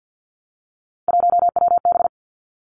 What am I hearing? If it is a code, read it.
Morse code, heard as a keyed tone, about 50 wpm, 700 Hz. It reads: 1WB
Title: Three Challenging Letter-Number Combinations 50wpm